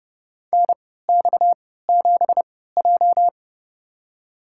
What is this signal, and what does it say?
Morse code, heard as a keyed tone, about 30 wpm, spaced as for 17 wpm, 700 Hz. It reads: NX7J
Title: USA Call Signs - 4 Characters - 3x Character Spacing - 30wpm